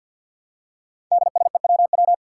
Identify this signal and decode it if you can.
Morse code, heard as a keyed tone, about 47 wpm, 700 Hz. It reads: DSEWW
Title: Five Challenging Letter-Number Combinations 50wpm